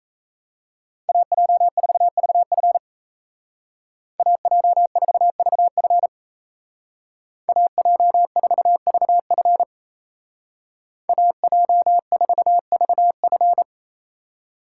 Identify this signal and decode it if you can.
Morse code, heard as a keyed tone, about 37 wpm, 700 Hz. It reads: AJ4VF AJ4VF AJ4VF AJ4VF